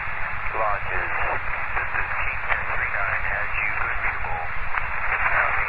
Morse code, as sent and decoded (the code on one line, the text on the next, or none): none